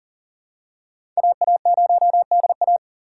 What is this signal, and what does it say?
Morse code, heard as a keyed tone, about 37 wpm, 700 Hz. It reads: AA0DA
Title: USA Call Signs - 4x Character Spacing - 40wpm